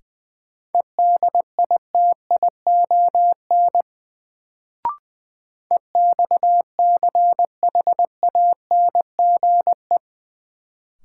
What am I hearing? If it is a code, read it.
Morse code, heard as a keyed tone, about 20 wpm, 700 Hz. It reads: EDITION EXCHANGE
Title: Sets of 5 Words - New Word Drill - 20wpm